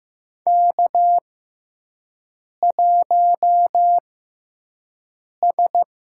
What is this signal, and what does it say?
Morse code, heard as a keyed tone, about 15 wpm, spaced as for 5 wpm, 700 Hz. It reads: K1S
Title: USA Call Signs - 5 Characters - 6x Character Spacing - 15wpm